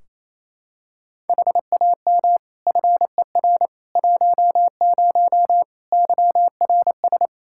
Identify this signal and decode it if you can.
Morse code, heard as a keyed tone, about 28 wpm, 700 Hz. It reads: HAM FER 10 YRS
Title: Practice QSO - Exchange 3 - More Info - 28wpm